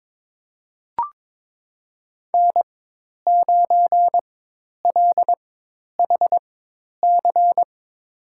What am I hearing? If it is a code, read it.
Morse code, heard as a keyed tone, about 21 wpm, spaced as for 10 wpm, 700 Hz. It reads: N9LHC
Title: USA Call Signs - 5 Characters - 4x Character Spacing - 22wpm